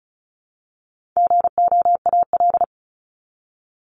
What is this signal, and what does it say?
Morse code, heard as a keyed tone, about 34 wpm, 700 Hz. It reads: GOAL
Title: Top 751-1000 Words 35wpm